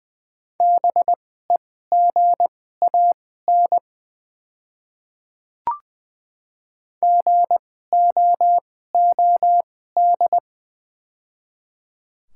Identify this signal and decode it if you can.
Morse code, heard as a keyed tone, about 20 wpm, spaced as for 15 wpm, 700 Hz. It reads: BEGAN GOOD
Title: 2x Character Spacing - Top 300 Words - 20wpm